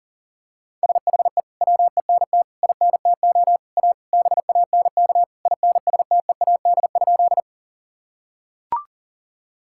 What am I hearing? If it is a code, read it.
Morse code, heard as a keyed tone, about 40 wpm, 700 Hz. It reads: SHE WENT INTO A BANK INSTEAD?
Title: Sentences from Top 1000 Words - Review Part 6 - 40wpm